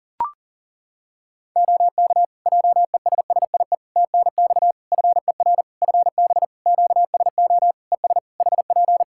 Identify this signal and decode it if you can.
Morse code, heard as a keyed tone, about 40 wpm, 700 Hz. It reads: OK JESSIE TNX FER FB QSO ES HP
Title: Practice QSO - Exchange 4 - Ending - 40wpm